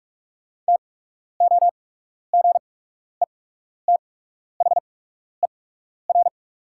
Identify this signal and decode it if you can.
Morse code, heard as a keyed tone, about 43 wpm, spaced as for 12 wpm, 700 Hz. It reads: TOGETHER
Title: 8x Character Spacing - Top 500 Words - 45wpm